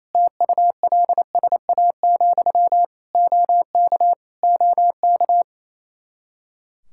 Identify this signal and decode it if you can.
Morse code, heard as a keyed tone, about 28 wpm, 700 Hz. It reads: TULSA, OK OK
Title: Practice QSO - Exchange 1 - Beginning - 28wpm